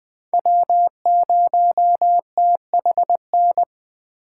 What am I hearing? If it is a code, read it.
Morse code, heard as a keyed tone, about 20 wpm, 700 Hz. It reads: W0THN